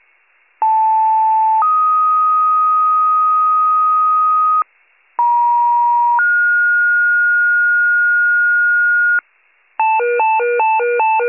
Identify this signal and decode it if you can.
Two-tone page, heard as a one-second tone, then a three-second tone, about 880 then 1250 Hz, about 950 then 1430 Hz, with noise. It is heard from 0.6 to 9.2 s.